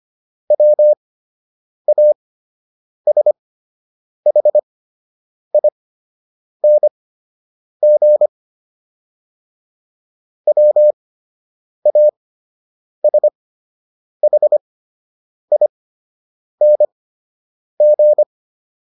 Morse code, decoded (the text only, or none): WASHING WASHING